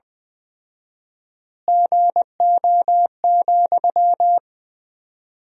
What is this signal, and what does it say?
Morse code, heard as a keyed tone, about 21 wpm, 700 Hz. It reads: GO,
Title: 4x Word-Spacing - Sentences from Top 400 Words - 20wpm